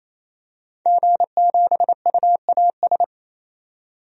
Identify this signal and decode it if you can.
Morse code, heard as a keyed tone, about 28 wpm, 700 Hz. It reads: G7UAS
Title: DXCC-100 Call Signs - 5 Characters - 28wpm